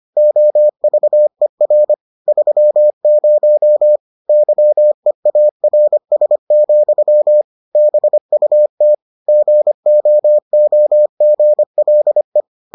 Morse code, heard as a keyed tone, about 25 wpm, 600 Hz. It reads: OVER 30 YEARS, BUT GOOGLE